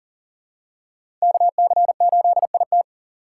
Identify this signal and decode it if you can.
Morse code, heard as a keyed tone, about 40 wpm, 700 Hz. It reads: KC8IT
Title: USA Call Signs - 2x Character Spacing - 40wpm